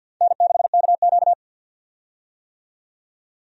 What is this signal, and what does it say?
Morse code, heard as a keyed tone, about 53 wpm, 700 Hz. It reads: N6KQ
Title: USA Call Signs - 4 Characters - 50wpm